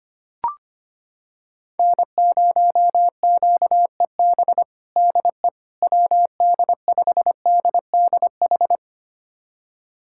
Morse code, heard as a keyed tone, about 25 wpm, 700 Hz. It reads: N0QEB DE WD5DDH
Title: USA Callsign DE Callsign 25wpm